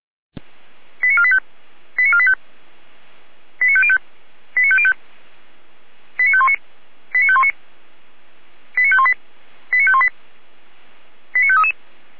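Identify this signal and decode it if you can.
Five-tone selective call (ZVEI), heard as quick runs of five tones, tones of about 70 ms. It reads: 89485 89485 89505 89505 89519 89519 89518 89518 89533
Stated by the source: ZVEI five-tone call 89020 received via SDR.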